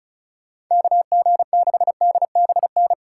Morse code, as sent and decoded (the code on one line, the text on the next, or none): -.- --. -.... -.. -... -.
KG6DBN